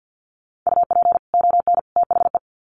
Morse code, heard as a keyed tone, about 50 wpm, 700 Hz. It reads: UP ON THE
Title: Sets of 3 Words 50wpm